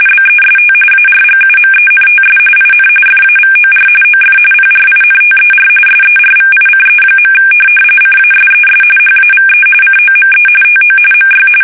RTTY, about 45.45 baud. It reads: BEST ARGUMENTS IN FAVOR OF PRINTER SERVICE. CASES HAVE COME TO MY
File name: fsk850